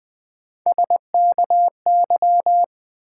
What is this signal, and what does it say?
Morse code, heard as a keyed tone, about 21 wpm, 700 Hz. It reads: SKY
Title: Sentences from Top 1200 Words - New Word Drill - 20wpm